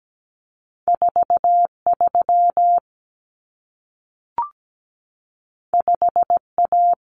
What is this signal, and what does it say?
Morse code, heard as a keyed tone, about 17 wpm, 700 Hz. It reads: 43 5A
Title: Common Cut Numbers - 2-Digit 17wpm